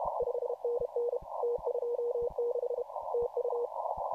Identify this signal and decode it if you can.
Morse code, heard as a keyed tone, about 33 wpm, 480 Hz, with noise. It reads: 5NN T26 TU